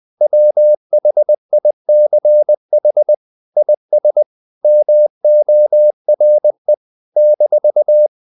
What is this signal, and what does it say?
Morse code, heard as a keyed tone, about 20 wpm, 600 Hz. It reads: WHICH IS MORE -
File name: CW / Practice / If20